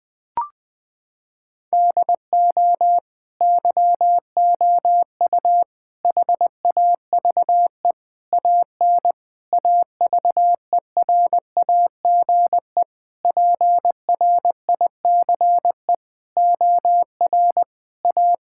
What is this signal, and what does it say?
Morse code, heard as a keyed tone, about 20 wpm, 700 Hz. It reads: DO YOU HAVE AN AVERAGE PRICE OR A
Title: Sets of 9 Words - 20wpm